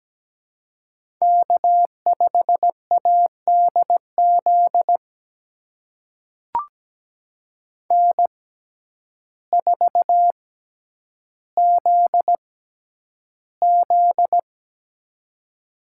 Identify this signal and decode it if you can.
Morse code, heard as a keyed tone, about 17 wpm, spaced as for 9 wpm, 700 Hz. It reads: K5ADZ N4ZZ
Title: USA Call Signs - 6x Character Spacing - 17wpm